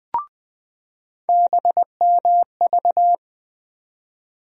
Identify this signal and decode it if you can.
Morse code, heard as a keyed tone, about 20 wpm, 700 Hz. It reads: BMV
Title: USA Call Sign Postfixes - 3 Letters 20wpm